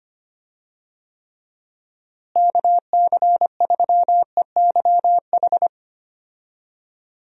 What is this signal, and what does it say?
Morse code, heard as a keyed tone, about 25 wpm, 700 Hz. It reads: KC3EYH
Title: USA Call Signs - 6 Characters - Repeated 2x - 25wpm